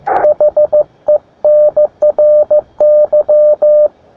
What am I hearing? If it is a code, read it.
Morse code, heard as a keyed tone, about 15 wpm, 600 Hz. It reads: HENRY
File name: attachments / sp4ipl 20m